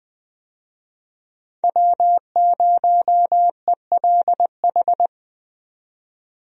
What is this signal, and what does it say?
Morse code, heard as a keyed tone, about 20 wpm, 700 Hz. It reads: W0ELH
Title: USA Call Signs - 5 Characters - 20wpm